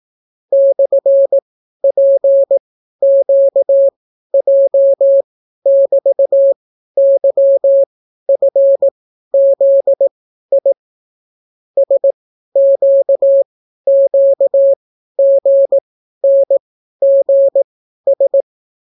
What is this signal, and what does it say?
Morse code, heard as a keyed tone, about 18 wpm, spaced as for 12 wpm, 550 Hz. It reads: /PQJ<BT>YFZI SQQGNGS